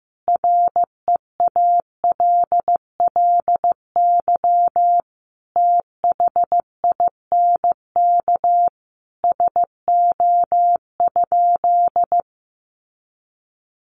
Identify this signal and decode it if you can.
Morse code, heard as a keyed tone, about 14 wpm, 700 Hz. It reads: REALLY THINK SO?
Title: Sentences from Top 600 Words 15wpm